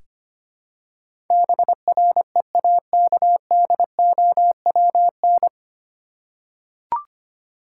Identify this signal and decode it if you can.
Morse code, heard as a keyed tone, about 25 wpm, 700 Hz. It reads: BREAKDOWN
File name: Nine Letter Words - 25wpm